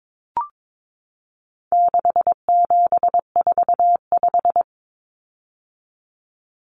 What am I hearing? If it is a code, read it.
Morse code, heard as a keyed tone, about 22 wpm, 700 Hz. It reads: 6745
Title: Numbers - 4-Digit 22wpm